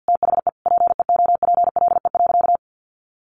Morse code, heard as a keyed tone, about 50 wpm, 700 Hz. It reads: THE PEOPLE.
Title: Sentences from Top 1500 Words 50wpm